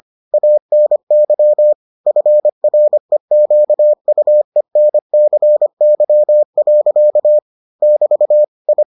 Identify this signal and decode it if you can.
Morse code, heard as a keyed tone, about 25 wpm, 600 Hz. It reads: ANY FREQUENCY. <BT> I